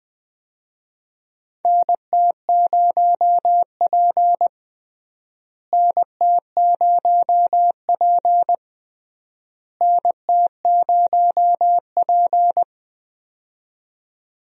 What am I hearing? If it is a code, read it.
Morse code, heard as a keyed tone, about 20 wpm, 700 Hz. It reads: NT0P NT0P NT0P
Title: USA Call Signs - 4 Characters - Repeated 3x - 20wpm